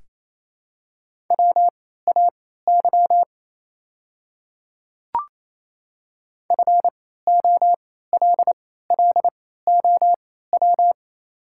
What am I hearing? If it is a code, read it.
Morse code, heard as a keyed tone, about 28 wpm, spaced as for 16 wpm, 700 Hz. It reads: WAY FOLLOW
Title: 3x Character Spacing - Top 200 Words - 28wpm